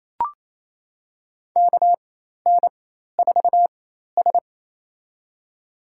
Morse code, heard as a keyed tone, about 27 wpm, spaced as for 13 wpm, 700 Hz. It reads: KN4S
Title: USA Call Signs - 4 Characters - 4x Character Spacing - 28wpm